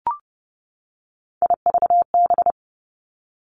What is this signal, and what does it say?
Morse code, heard as a keyed tone, about 30 wpm, 700 Hz. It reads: IVB
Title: Three Challenging Letter-Number Combinations 30wpm